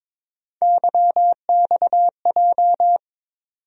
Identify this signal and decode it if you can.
Morse code, heard as a keyed tone, about 22 wpm, 700 Hz. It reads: YXJ
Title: USA Call Sign Postfixes - 3 Letters 22wpm